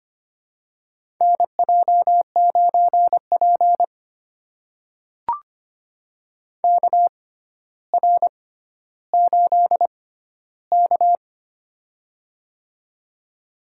Morse code, heard as a keyed tone, about 25 wpm, spaced as for 13 wpm, 700 Hz. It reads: NJ9P KR8K
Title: USA Call Signs - 4 Characters - 6x Character Spacing - 25wpm